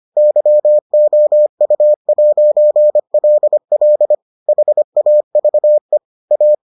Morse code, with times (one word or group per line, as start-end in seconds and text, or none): 0.2-4.1 YOU'LL
4.5-6.0 HAVE
6.3-6.5 A